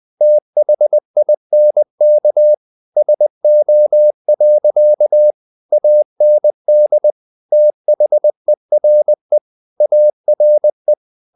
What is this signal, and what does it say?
Morse code, heard as a keyed tone, about 20 wpm, 600 Hz. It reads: THINK SO. AND THERE ARE
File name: the old man and the sea / the old man and the sea part 1 20wpm